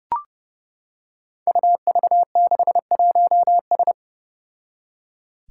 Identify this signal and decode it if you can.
Morse code, heard as a keyed tone, about 30 wpm, 700 Hz. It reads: UV61S
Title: Five Challenging Letter-Number Combinations 30wpm